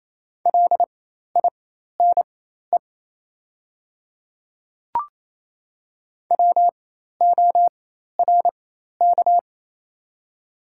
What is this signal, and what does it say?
Morse code, heard as a keyed tone, about 28 wpm, spaced as for 13 wpm, 700 Hz. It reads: LINE WORK
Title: 4x Character Spacing - Top 200 Words - 28wpm